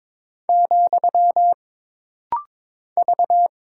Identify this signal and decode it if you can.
Morse code, heard as a keyed tone, about 22 wpm, 700 Hz. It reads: , V
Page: This rapid-fire timing is geared to push your ICR (Instant Character Recognition) skills.